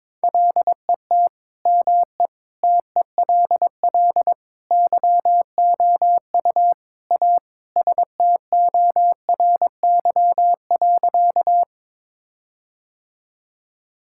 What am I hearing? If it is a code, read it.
Morse code, heard as a keyed tone, about 22 wpm, 700 Hz. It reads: LET ME TELL YOU A STORY.
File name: practice / 22wpm / Sentences with 4-Word Phrases - 22wpm